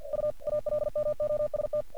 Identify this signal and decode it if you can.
Morse code, heard as a keyed tone, about 46 wpm, 600 Hz. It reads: U ALMOST
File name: archived / CRSnet